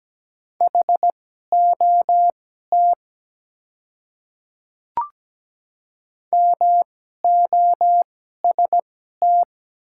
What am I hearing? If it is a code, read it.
Morse code, heard as a keyed tone, about 17 wpm, spaced as for 12 wpm, 700 Hz. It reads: HOT MOST
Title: 2x Character Spacing - Top 100 Words - 17wpm